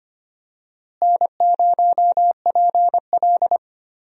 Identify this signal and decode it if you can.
Morse code, heard as a keyed tone, about 25 wpm, 700 Hz. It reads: N0PL
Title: USA Call Signs - 4 Characters - Repeated 2x - 25wpm